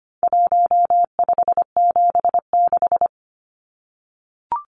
Morse code, heard as a keyed tone, about 25 wpm, 700 Hz. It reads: 1576
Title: Numbers - 4-Digit 25wpm